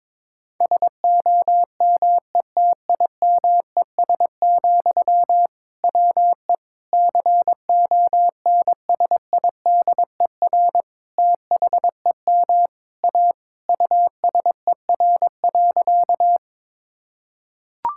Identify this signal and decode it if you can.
Morse code, heard as a keyed tone, about 22 wpm, 700 Hz. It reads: SOMETIMES, WE CONSIDER THEM A USER.
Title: Sentences from Top 1000 Words - Review Part 5 - 22wpm